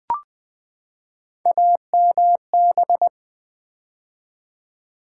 Morse code, heard as a keyed tone, about 20 wpm, 700 Hz. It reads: AMB
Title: Build A Word - 3 Letters At A Time - 9-Letter Words - 20wpm